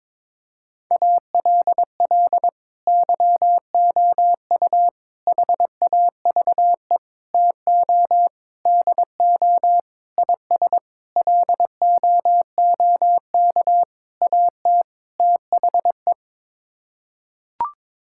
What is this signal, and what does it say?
Morse code, heard as a keyed tone, about 21 wpm, 700 Hz. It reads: ALL YOU HAVE TO DO IS LOOK AT THE
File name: Sets of 9 Words - 22wpm